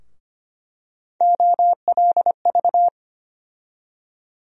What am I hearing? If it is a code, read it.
Morse code, heard as a keyed tone, about 25 wpm, 700 Hz. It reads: OLV
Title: USA Call Sign Postfixes - 3 Letters 25wpm